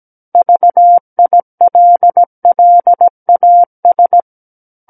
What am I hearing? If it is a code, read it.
Morse code, heard as a keyed tone, about 17 wpm, 700 Hz. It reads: VILLAS
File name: Book One / 08-Chapter 8 - 17 WPM